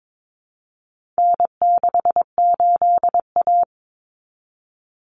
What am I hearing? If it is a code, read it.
Morse code, heard as a keyed tone, about 22 wpm, 700 Hz. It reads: N68A
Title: Common Cut Numbers - 4-Digit 22wpm